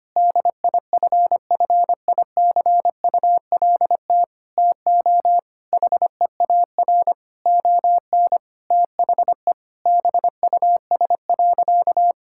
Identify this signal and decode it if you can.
Morse code, heard as a keyed tone, about 25 wpm, 700 Hz. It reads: DIFFICULT TO HEAR ON THE BUS.
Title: Sentences from Top 1000 Words - Review Part 6 - 25wpm